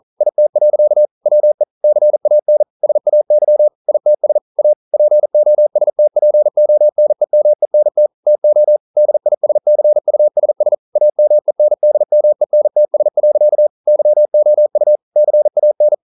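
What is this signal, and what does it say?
Morse code, heard as a keyed tone, about 41 wpm, 600 Hz. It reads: IT. WE CAN SAY ITS A POSTPONEMENT TO DISCUSS AMENDMENTS. YOU CAN
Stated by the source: Morse code practice at 41 wpm, from CW med Gnister.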